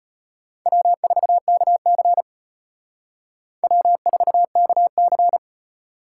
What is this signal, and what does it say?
Morse code, heard as a keyed tone, about 32 wpm, 700 Hz. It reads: W4KC W4KC